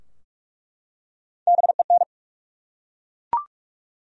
Morse code, heard as a keyed tone, about 44 wpm, 700 Hz. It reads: BEN
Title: USA Common Names - Max 4 Letters - 45wpm